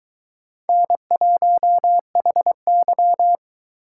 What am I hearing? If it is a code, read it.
Morse code, heard as a keyed tone, about 23 wpm, 700 Hz. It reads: N1HY